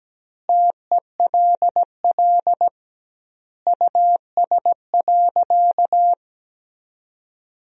Morse code, heard as a keyed tone, about 17 wpm, 700 Hz. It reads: TELL US.
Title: 2x Word-Spacing - Sentences from Top 400 Words - 17wpm